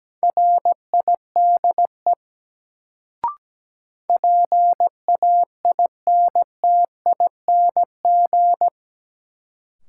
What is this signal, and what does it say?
Morse code, heard as a keyed tone, about 17 wpm, 700 Hz. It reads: RIDE PAINTING
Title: Sentences from Top 1200 Words - New Word Drill Alt - 17wpm